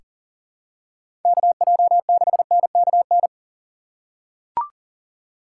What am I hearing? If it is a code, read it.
Morse code, heard as a keyed tone, about 40 wpm, 700 Hz. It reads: KJ6NKN
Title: USA Call Signs - 2x Character Spacing - 40wpm